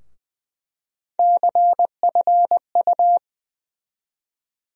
Morse code, heard as a keyed tone, about 20 wpm, 700 Hz. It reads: CFU
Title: USA Call Sign Postfixes - 3 Letters 20wpm